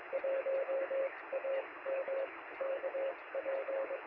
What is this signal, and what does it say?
Morse code, heard as a keyed tone, about 23 wpm, spaced as for 19 wpm, 560 Hz, with radio noise. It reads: 1 AM KP
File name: sm7lcb / 2139z oh3lwp 3cm